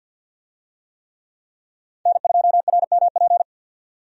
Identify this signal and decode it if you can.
Morse code, heard as a keyed tone, about 52 wpm, 700 Hz. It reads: N2RMP
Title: USA Call Signs - 5 Characters - 50wpm